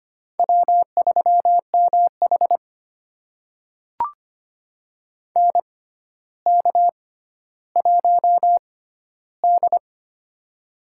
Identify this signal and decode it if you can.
Morse code, heard as a keyed tone, about 25 wpm, spaced as for 13 wpm, 700 Hz. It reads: W3MH NK1D
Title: USA Call Signs - 4 Characters - 6x Character Spacing - 25wpm